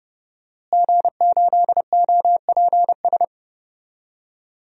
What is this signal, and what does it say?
Morse code, heard as a keyed tone, about 31 wpm, 700 Hz. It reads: G8OPS
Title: DXCC-100 Call Signs - 5 Characters - Repeated 2x - 30wpm